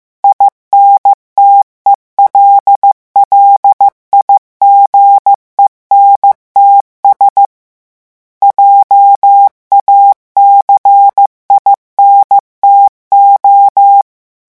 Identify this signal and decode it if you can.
Morse code, heard as a keyed tone, about 15 wpm, 800 Hz. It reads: INTELLIGENTS JACINTO
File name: morse mp3 / revision9 citations humoristiques vitesse 14 mots minute